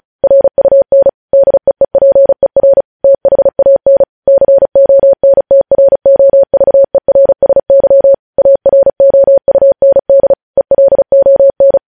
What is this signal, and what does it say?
Morse code, heard as a keyed tone, about 36 wpm, 550 Hz. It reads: RUN DEEPER THAN CONTROVERSY AROUND ELON
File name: podcast / News headlines-1x-35x35